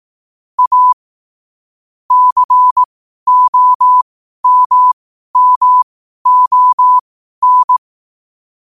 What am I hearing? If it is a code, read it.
Morse code, heard as a keyed tone, about 18 wpm, spaced as for 12 wpm, 1000 Hz. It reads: A COMMON
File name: qotdfiles / 12wpm